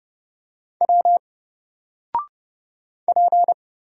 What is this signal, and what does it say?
Morse code, heard as a keyed tone, about 30 wpm, 700 Hz. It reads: W P
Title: Single Letter-Number 30wpm